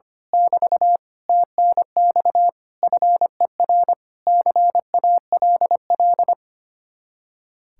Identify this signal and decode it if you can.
Morse code, heard as a keyed tone, about 24 wpm, 700 Hz. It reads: <BT> TNX FER CALL
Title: Speed-Racing Common QSO Phrases Encore - 25wpm